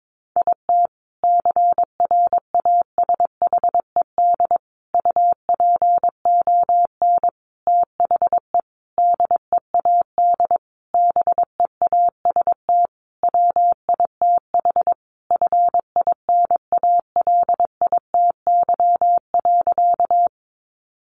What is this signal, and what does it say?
Morse code, heard as a keyed tone, about 22 wpm, 700 Hz. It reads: IT CRASHED UPON THE DEAD BEAST WITH FINALITY.
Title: A Sound Of Thunder 22wpm